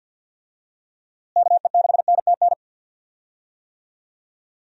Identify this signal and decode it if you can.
Morse code, heard as a keyed tone, about 50 wpm, 700 Hz. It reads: KE6NTN
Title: USA Call Signs - 6 Characters - Repeated 3x - 50wpm